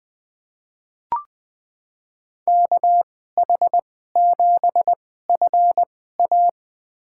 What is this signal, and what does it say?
Morse code, heard as a keyed tone, about 20 wpm, spaced as for 14 wpm, 700 Hz. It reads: KH7FA
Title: USA Call Signs - 5 Characters - 2x Character Spacing - 20wpm